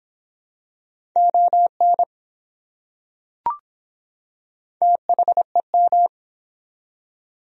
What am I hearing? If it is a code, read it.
Morse code, heard as a keyed tone, about 24 wpm, 700 Hz. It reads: ON THEM